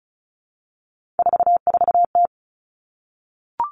Morse code, heard as a keyed tone, about 36 wpm, 700 Hz. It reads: 44T
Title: Three Challenging Letter-Number Combinations 35wpm